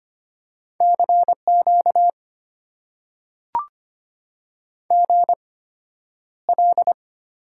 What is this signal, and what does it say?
Morse code, heard as a keyed tone, about 25 wpm, spaced as for 11 wpm, 690 Hz. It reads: CQ GL
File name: Top 90 QSO Elements - 8x Character Spacing - 25wpm